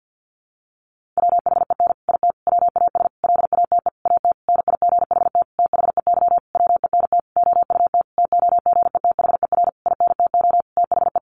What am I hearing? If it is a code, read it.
Morse code, heard as a keyed tone, about 50 wpm, 700 Hz. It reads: WHEN IT WAS LATE AT NIGHT THEY WENT OUT TOGETHER INTO THE
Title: Sentences from Top 600 Words 50wpm